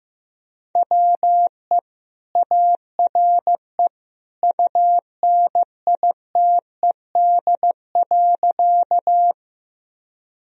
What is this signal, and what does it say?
Morse code, heard as a keyed tone, about 15 wpm, 700 Hz. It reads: WE ARE UNITED.
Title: Sentences from Top 1000 Words - Review Part 4 - 15wpm